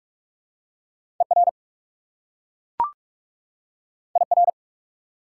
Morse code, heard as a keyed tone, about 44 wpm, 690 Hz. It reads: ER IR
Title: Prefixes and Suffixes 45wpm